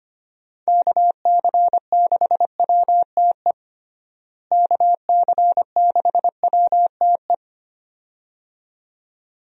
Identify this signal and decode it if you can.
Morse code, heard as a keyed tone, about 25 wpm, 700 Hz. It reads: KC6WTE KC6WTE
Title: USA Call Signs - 6 Characters - Repeated 2x - 25wpm